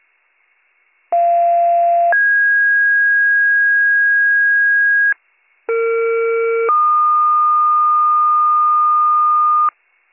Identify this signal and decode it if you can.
Two-tone page, heard as a one-second tone, then a three-second tone, about 690 then 1650 Hz, about 460 then 1220 Hz. It is heard from 1.1 to 9.7 s.